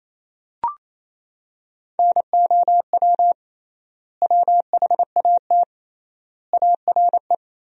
Morse code, heard as a keyed tone, about 28 wpm, 700 Hz. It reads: NOW WHAT ARE
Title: 3x Word-Spacing - Sentences from Top 100 Words - 28wpm